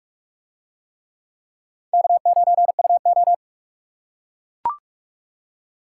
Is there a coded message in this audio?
Yes: Morse code from 1.9 to 3.3 s, heard as a keyed tone, about 44 wpm, 700 Hz. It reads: K9UO